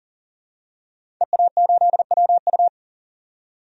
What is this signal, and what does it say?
Morse code, heard as a keyed tone, about 40 wpm, 700 Hz. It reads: EA8WU
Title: DXCC-100 Call Signs - 5 Characters - 40wpm